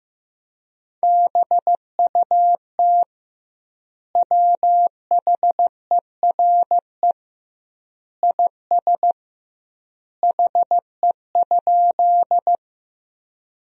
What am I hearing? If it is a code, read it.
Morse code, heard as a keyed tone, about 15 wpm, 700 Hz. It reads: BUT WHERE IS HE?
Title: 2x Word-Spacing - Sentences from Top 200 Words - 15wpm